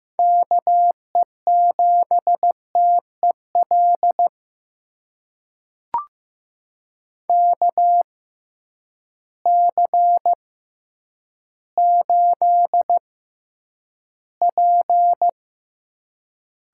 Morse code, heard as a keyed tone, about 15 wpm, spaced as for 9 wpm, 700 Hz. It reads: KE7TEL KC8P
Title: USA Call Signs - 6x Character Spacing - 15wpm